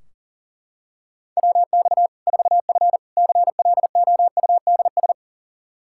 Morse code, heard as a keyed tone, about 40 wpm, 700 Hz. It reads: WX 4F CLOUDS
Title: Practice QSO - Exchange 2 - Info - 40wpm